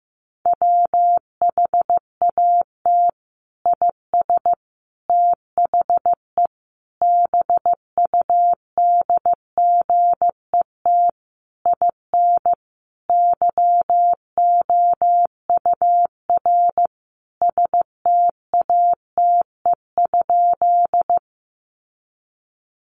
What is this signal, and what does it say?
Morse code, heard as a keyed tone, about 15 wpm, 700 Hz. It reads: WHAT IS THE BUDGET IN YOUR STATE?
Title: Sentences from Top 1200 Words 15wpm